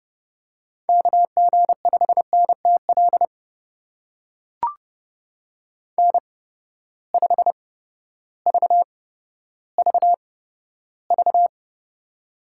Morse code, heard as a keyed tone, about 30 wpm, spaced as for 14 wpm, 700 Hz. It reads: KG5NTL N5VVV
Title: USA Call Signs - 8x Character Spacing - 30wpm